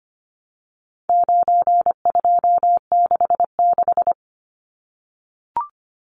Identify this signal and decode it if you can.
Morse code, heard as a keyed tone, about 25 wpm, 700 Hz. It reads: 9266
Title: Numbers - 4-Digit 25wpm